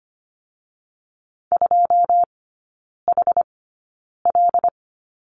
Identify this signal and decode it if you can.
Morse code, heard as a keyed tone, about 25 wpm, spaced as for 9 wpm, 700 Hz. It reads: 2HL
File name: Lesson 15e - Number 5 - Three Characters 25wpm